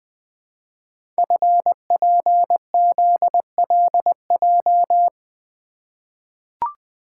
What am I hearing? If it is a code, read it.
Morse code, heard as a keyed tone, about 20 wpm, 700 Hz. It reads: FPZLJ
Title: Code Groups - 5 Letter:Numbers - 20wpm